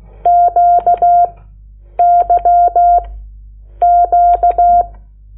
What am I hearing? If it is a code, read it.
Morse code, heard as a keyed tone, about 16 wpm, 700 Hz. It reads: Q Y Q
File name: Lecciones / CLASE Nº8